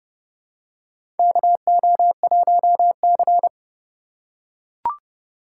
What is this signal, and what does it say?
Morse code, heard as a keyed tone, about 30 wpm, 700 Hz. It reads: KO1C
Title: USA Call Signs - 4 Characters - 4x Character Spacing - 30wpm